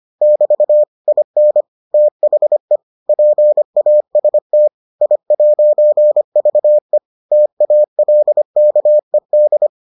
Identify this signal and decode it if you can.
Morse code, heard as a keyed tone, about 25 wpm, 600 Hz. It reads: <BT> IN THE PAST I'VE TALKED